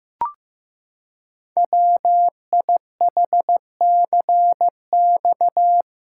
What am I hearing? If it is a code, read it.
Morse code, heard as a keyed tone, about 15 wpm, 700 Hz. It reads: WIHCX
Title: Code Groups - 5 Letters - 15wpm